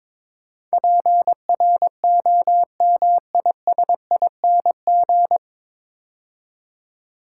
Morse code, heard as a keyed tone, about 22 wpm, 700 Hz. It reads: PROMISING
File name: 9 Letter Words - First 3 Letters Given - 22wpm